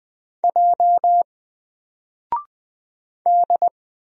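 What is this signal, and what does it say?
Morse code, heard as a keyed tone, about 20 wpm, 700 Hz. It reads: J D
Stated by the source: Single letter and number Morse drill at 20 wpm, from N4LCD.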